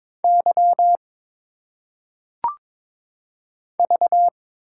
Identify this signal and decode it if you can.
Morse code, heard as a keyed tone, about 22 wpm, 700 Hz. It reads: Y V
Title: Single Letters - Phonetic Pronunciation 22wpm